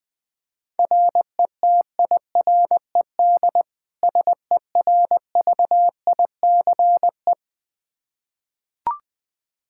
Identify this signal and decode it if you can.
Morse code, heard as a keyed tone, about 21 wpm, 700 Hz. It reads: RETIRED SERVICE
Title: Practice QSO - Exchange 3 - More Info - 20wpm